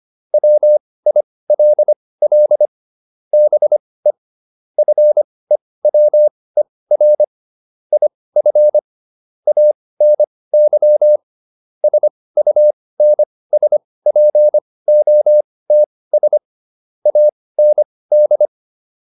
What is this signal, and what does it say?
Morse code, heard as a keyed tone, about 25 wpm, spaced as for 18 wpm, 600 Hz. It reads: WILL BE FEWER IF ANY SUNSPOTS AND